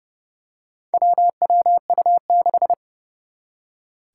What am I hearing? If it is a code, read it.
Morse code, heard as a keyed tone, about 30 wpm, 700 Hz. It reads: WWU6
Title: Four Challenging Letter-Number Combinations 30wpm